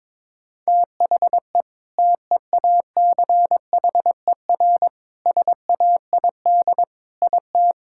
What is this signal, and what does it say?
Morse code, heard as a keyed tone, about 22 wpm, 700 Hz. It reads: THE TEACHER SAID IT
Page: morse-code-ninja-practice-22wpm